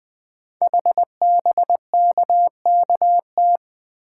Morse code, heard as a keyed tone, about 20 wpm, 700 Hz. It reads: HBKKT
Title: Code Groups - 5 Letters - 20wpm